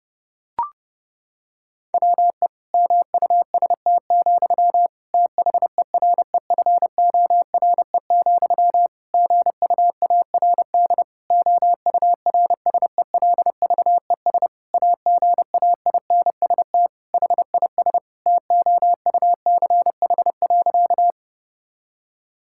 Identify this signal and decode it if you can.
Morse code, heard as a keyed tone, about 30 wpm, 700 Hz. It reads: WE MUST, THEREFORE, GUARD OURSELVES AGAINST HIS TOUCH.
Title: Sentences from Top 1600 Words 30wpm